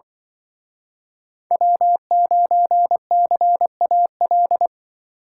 Morse code, heard as a keyed tone, about 25 wpm, 700 Hz. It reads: W9CAL
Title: Speed Racing - USA Call Signs 20wpm